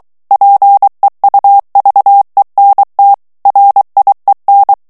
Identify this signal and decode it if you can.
Morse code, heard as a keyed tone, about 23 wpm, 800 Hz. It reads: PEUVENT RIEN
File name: CW / FICHIERS-MP3 / FRANCAIS TEXTE CLAIR 2 VITESSE 22 MOTS MINUTES